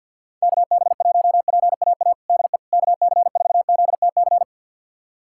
Morse code, heard as a keyed tone, about 52 wpm, 700 Hz. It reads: KB1PAA DE KC4ZTC
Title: USA Callsign DE Callsign - 6 Characters - 50wpm